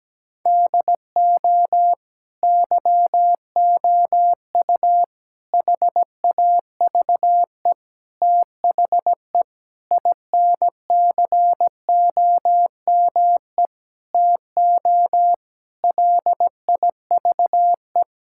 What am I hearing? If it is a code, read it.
Morse code, heard as a keyed tone, about 17 wpm, 700 Hz. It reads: DO YOU HAVE THE INCOME TO LIVE
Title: Sentences from Top 1000 Words - Review Part 5 - 17wpm